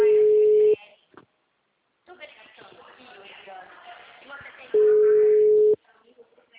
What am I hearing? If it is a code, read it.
Ringback tone (Europe, 425 Hz), with speech.